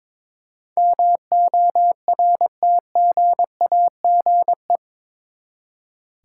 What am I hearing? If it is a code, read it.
Morse code, heard as a keyed tone, about 22 wpm, 700 Hz. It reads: MORTGAGE
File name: practice / 22wpm / Eight Letter Words - 22wpm